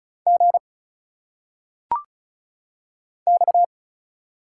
Morse code, heard as a keyed tone, about 34 wpm, 700 Hz. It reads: G X
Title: Single Letters - Phonetic Pronunciation 35wpm